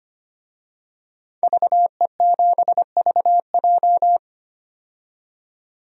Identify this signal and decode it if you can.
Morse code, heard as a keyed tone, about 25 wpm, 700 Hz. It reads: VE7VJ
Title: DXCC-100 Call Signs - 5 Characters - 25wpm